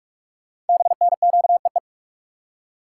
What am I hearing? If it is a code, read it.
Morse code, heard as a keyed tone, about 44 wpm, 700 Hz. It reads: BNQEE
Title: Code Groups - 5 Letters - 45wpm